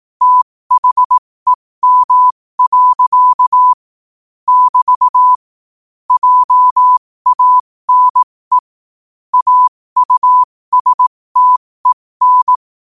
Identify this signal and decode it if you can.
Morse code, heard as a keyed tone, about 18 wpm, spaced as for 15 wpm, 1000 Hz. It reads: THEM. <BT> JANE AUSTEN
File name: qotdfiles / 15wpm